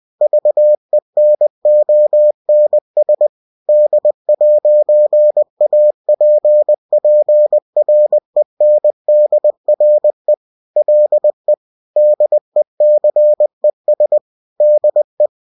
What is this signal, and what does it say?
Morse code, heard as a keyed tone, about 20 wpm, 600 Hz. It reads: VENONS D'APPRENDRE LE DECES DE